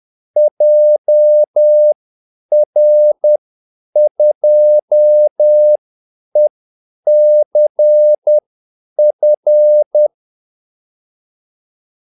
Morse code, heard as a keyed tone, about 10 wpm, spaced as for 8 wpm, 600 Hz. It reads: JR2ECF
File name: Callsigns-10-8wpm